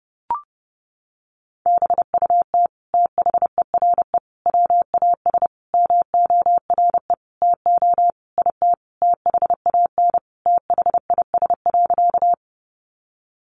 Morse code, heard as a keyed tone, about 28 wpm, 700 Hz. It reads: BUT THERE WAS MORE TO IT THAN THIS.
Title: Sentences from Top 100 Words 30wpm